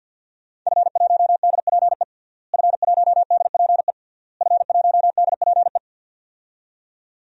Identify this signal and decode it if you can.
Morse code, heard as a keyed tone, about 52 wpm, 700 Hz. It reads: F1DPE F1DPE F1DPE
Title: DXCC-100 Call Signs - 5 Characters - Repeated 3x - 50wpm